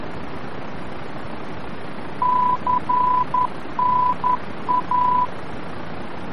Morse code, heard as a keyed tone, about 10 wpm, 1000 Hz, with noise. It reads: CNA
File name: france / mx / CNA-VOR-F